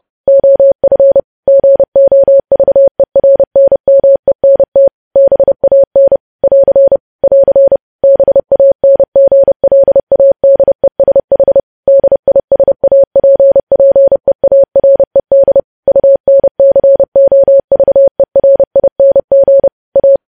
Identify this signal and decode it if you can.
Morse code, heard as a keyed tone, about 30 wpm, 550 Hz. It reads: OF GOVERNMENT BAN <AR> <AR> BANGLADESH DISAPPEARED UNCOVERING A
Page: news headlines in morse code at 30 wpm